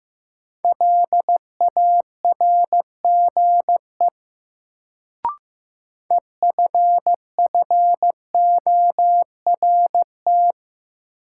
Morse code, heard as a keyed tone, about 15 wpm, 700 Hz. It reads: LARGE EFFORT
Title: Sets of 4 Words - New Word Drill - 15wpm